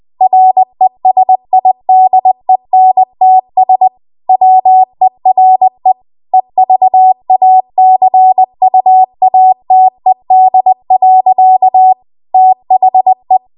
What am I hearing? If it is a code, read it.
Morse code, heard as a keyed tone, about 20 wpm, 750 Hz. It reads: RESIDENTS WERE EVACUATED. THE
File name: Archive / 20 WPM / 20WPM